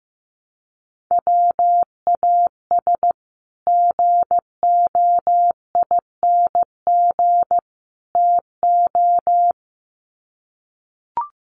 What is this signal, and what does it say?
Morse code, heard as a keyed tone, about 15 wpm, 700 Hz. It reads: WAS GOING TO
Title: Sets of 3 Words 15wpm